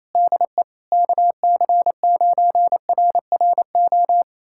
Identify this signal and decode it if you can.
Morse code, heard as a keyed tone, about 28 wpm, 700 Hz. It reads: DE KC9RRO
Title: USA Callsign DE Callsign - 6 Characters - 28wpm